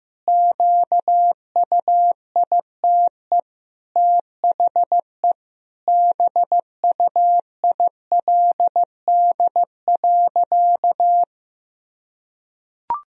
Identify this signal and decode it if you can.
Morse code, heard as a keyed tone, about 16 wpm, 700 Hz. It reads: QUITE THE BUILD.
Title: Sentences from Top 1000 Words - Review Part 1 - 15wpm